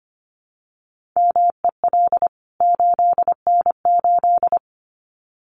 Morse code, heard as a keyed tone, about 25 wpm, 700 Hz. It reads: MEL 8N8
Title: CWT Exchange 25wpm